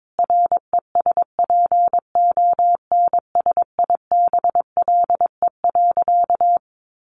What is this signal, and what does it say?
Morse code, heard as a keyed tone, about 22 wpm, 700 Hz. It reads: RESPONSIBLE.
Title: A Sound Of Thunder 22wpm